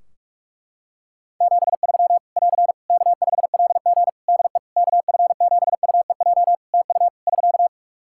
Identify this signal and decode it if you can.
Morse code, heard as a keyed tone, about 42 wpm, 700 Hz. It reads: 73 <AR> K5LG DE KF7UEJ TU <SK>
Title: K1USN SST QSO Practice 45wpm